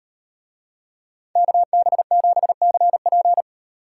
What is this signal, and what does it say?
Morse code, heard as a keyed tone, about 38 wpm, 700 Hz. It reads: KB7CP